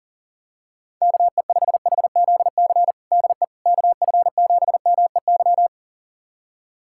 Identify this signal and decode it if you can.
Morse code, heard as a keyed tone, about 40 wpm, 700 Hz. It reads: KE5HZC DE KF7MEY
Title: USA Callsign DE Callsign - 6 Characters - 40wpm